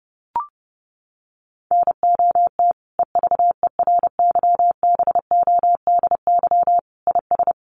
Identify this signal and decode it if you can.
Morse code, heard as a keyed tone, about 30 wpm, 700 Hz. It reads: NOT EVERYBODY IS
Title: Sentences from Top 1400 Words 30wpm